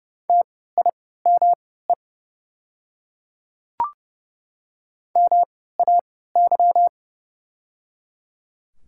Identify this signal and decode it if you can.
Morse code, heard as a keyed tone, about 30 wpm, spaced as for 17 wpm, 700 Hz. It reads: TIME MAY
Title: 3x Character Spacing - Top 100 Words - 30wpm